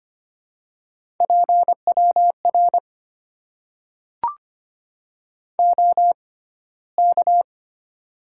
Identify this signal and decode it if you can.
Morse code, heard as a keyed tone, about 25 wpm, spaced as for 15 wpm, 700 Hz. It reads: PWR OK
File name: Top 90 QSO Elements - 6x Character Spacing - 25wpm